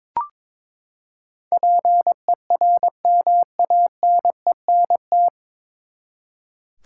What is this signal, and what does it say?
Morse code, heard as a keyed tone, about 22 wpm, 700 Hz. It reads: PERMANENT
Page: Morse Code Ninja practice 22wpm